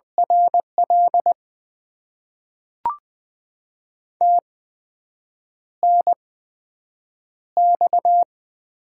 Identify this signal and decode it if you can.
Morse code, heard as a keyed tone, about 20 wpm, spaced as for 7 wpm, 700 Hz. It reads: RL TNX
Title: Top 20 QSO Elements - 8x Character Spacing - 20wpm